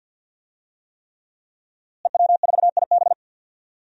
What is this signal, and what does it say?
Morse code, heard as a keyed tone, about 52 wpm, 700 Hz. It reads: EW4IB
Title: DXCC-100 Call Signs - 5 Characters - Repeated 3x - 50wpm